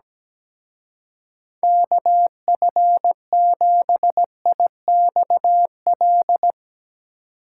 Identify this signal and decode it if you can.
Morse code, heard as a keyed tone, about 17 wpm, 700 Hz. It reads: KF7IXL
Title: Speed Racing - USA 2x3 Call Signs 17wpm